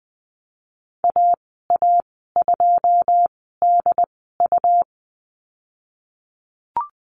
Morse code, heard as a keyed tone, about 19 wpm, spaced as for 14 wpm, 700 Hz. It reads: AA2DU
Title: 122 Lesson 20g - Number 3 - Call Signs 20wpm